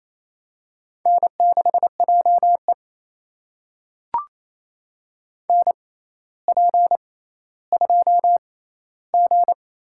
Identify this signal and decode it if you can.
Morse code, heard as a keyed tone, about 28 wpm, spaced as for 14 wpm, 700 Hz. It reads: N6JE NP2G
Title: USA Call Signs - 4 Characters - 6x Character Spacing - 28wpm